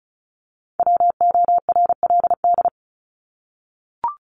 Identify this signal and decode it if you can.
Morse code, heard as a keyed tone, about 34 wpm, 700 Hz. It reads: WORLD
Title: Top 200 Words 35wpm